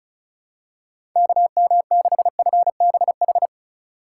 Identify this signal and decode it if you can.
Morse code, heard as a keyed tone, about 34 wpm, 700 Hz. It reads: KM6FBH
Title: USA Call Signs - 4x Character Spacing - 35wpm